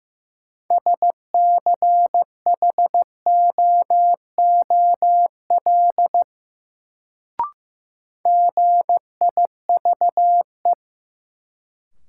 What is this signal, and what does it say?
Morse code, heard as a keyed tone, about 15 wpm, 700 Hz. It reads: SCHOOL GIVE
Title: Sets of 2 Words - New Word Drill - 15wpm